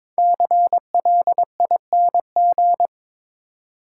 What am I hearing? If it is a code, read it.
Morse code, heard as a keyed tone, about 22 wpm, 690 Hz. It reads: CLING
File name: 5 Letter Words - First 3 Letters Given - 22wpm